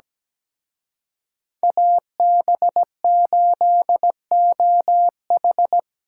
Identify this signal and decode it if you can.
Morse code, heard as a keyed tone, about 20 wpm, 700 Hz. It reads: AB8OH
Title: Speed Racing - USA 2x2 Call Signs 17wpm